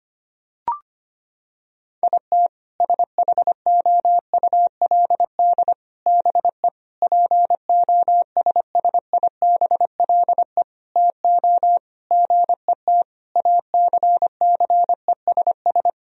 Morse code, heard as a keyed tone, about 25 wpm, 700 Hz. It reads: IT SHOULD BE POSSIBLE TO GET ACCESS
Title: Sentences from Top 1000 Words - Review Part 4 - 25wpm